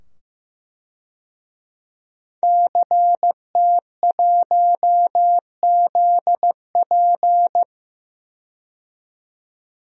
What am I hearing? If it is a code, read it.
Morse code, heard as a keyed tone, about 15 wpm, 700 Hz. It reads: CT1ZP
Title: DXCC-100 Call Signs - 5 Characters - 15wpm